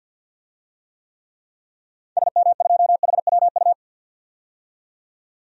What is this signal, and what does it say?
Morse code, heard as a keyed tone, about 52 wpm, 700 Hz. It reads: SM2HWU
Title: DXCC-100 Call Signs - 6 Characters - 50wpm